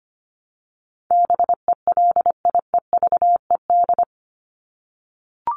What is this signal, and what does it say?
Morse code, heard as a keyed tone, about 25 wpm, 700 Hz. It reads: BELIEVED
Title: ED Suffix - Top 100 Words with Base Words 25wpm